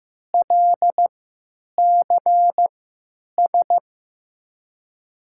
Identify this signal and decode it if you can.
Morse code, heard as a keyed tone, about 15 wpm, spaced as for 9 wpm, 700 Hz. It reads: LCS
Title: USA Call Signs - 6 Characters - 3x Character Spacing - 15wpm